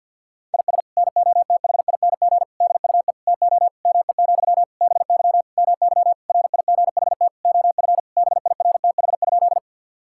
Morse code, heard as a keyed tone, about 51 wpm, 700 Hz. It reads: IS NOTHING DUE TO ME, BY MY RIGHT OF BIRTH?
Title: Sentences from Top 1000 Words - Review Part 6 - 50wpm